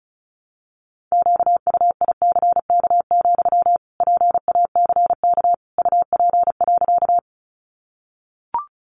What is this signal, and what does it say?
Morse code, heard as a keyed tone, about 34 wpm, 700 Hz. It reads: QUICK, PACK UP.
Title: Sentences from Top 1300 Words 35wpm